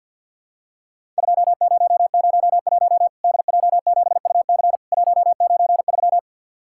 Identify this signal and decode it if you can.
Morse code, heard as a keyed tone, about 50 wpm, 700 Hz. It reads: 2001 DJ7UC 193
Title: Worked All Europe DX Contest - QTC Exchange - 50wpm